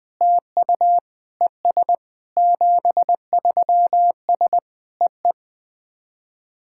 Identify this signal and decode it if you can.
Morse code, heard as a keyed tone, about 21 wpm, 700 Hz. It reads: TU ES 73S EE
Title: SOTA QSO Practice 20wpm